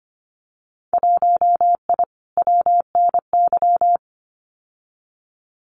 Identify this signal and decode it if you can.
Morse code, heard as a keyed tone, about 25 wpm, 700 Hz. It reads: 1I WNY
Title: Winter Field Day Exchanges Encore 25wpm